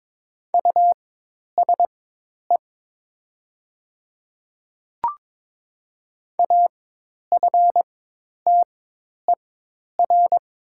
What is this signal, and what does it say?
Morse code, heard as a keyed tone, about 22 wpm, spaced as for 10 wpm, 700 Hz. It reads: USE AFTER
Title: 4x Character Spacing - Top 200 Words - 22wpm